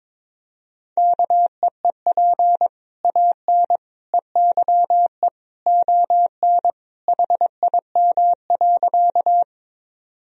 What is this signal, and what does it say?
Morse code, heard as a keyed tone, about 22 wpm, 700 Hz. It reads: KEEP AN EYE ON HIM.
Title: Sentences from Top 1000 Words - Review Part 1 - 22wpm